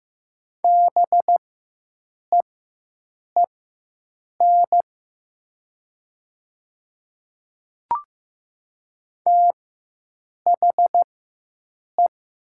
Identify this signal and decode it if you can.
Morse code, heard as a keyed tone, about 15 wpm, spaced as for 7 wpm, 700 Hz. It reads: BEEN THE
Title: 4x Character Spacing - Top 100 Words - 15wpm